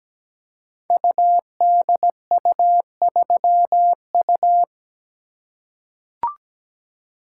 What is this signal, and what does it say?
Morse code, heard as a keyed tone, about 17 wpm, 700 Hz. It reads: UDU3U
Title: Code Groups - 5 Letter:Numbers - 17wpm